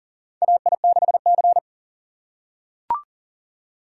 Morse code, heard as a keyed tone, about 40 wpm, 700 Hz. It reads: AI6C
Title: USA Call Signs - 4x Character Spacing - 40wpm